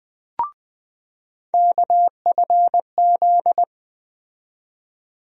Morse code, heard as a keyed tone, about 20 wpm, 700 Hz. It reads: KFZ
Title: USA Call Sign Postfixes - 3 Letters 20wpm